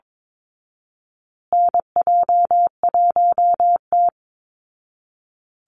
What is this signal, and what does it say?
Morse code, heard as a keyed tone, about 22 wpm, 690 Hz. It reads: NJ1T
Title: Call Signs Repeated 3x - 22wpm